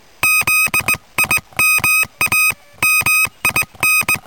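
Morse code, heard as a keyed tone, about 19 wpm, 2550 Hz, with noise. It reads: ZIMA MIN
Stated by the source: QRP contact between UA1CEG and UA3YLP using homemade gear on 14060 kHz during a QRP round table, with heavy interference from a contest.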